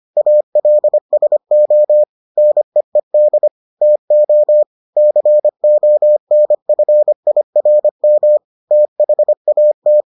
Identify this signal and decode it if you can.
Morse code, heard as a keyed tone, about 25 wpm, 600 Hz. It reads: ALSO NEED TO CONFIRM THAT